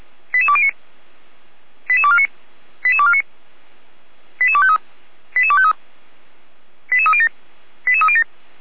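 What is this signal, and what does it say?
Five-tone selective call (ZVEI), heard as quick runs of five tones, tones of about 70 ms. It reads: 88209 88249 88249 88253 88253 88307 88307
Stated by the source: ZVEI five-tone selective call received by SDR, on the server of the Orlamünde fire brigade (Feuerwehr Orlamünde).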